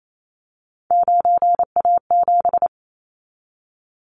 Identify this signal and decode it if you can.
Morse code, heard as a keyed tone, about 28 wpm, 700 Hz. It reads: 9A7
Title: Callsign Prefixes for Top 25 DXCC - 28wpm